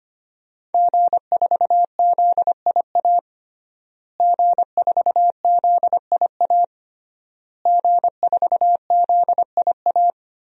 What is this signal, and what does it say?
Morse code, heard as a keyed tone, about 25 wpm, 700 Hz. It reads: G4ZIA G4ZIA G4ZIA
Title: DXCC-100 Call Signs - 5 Characters - Repeated 3x - 25wpm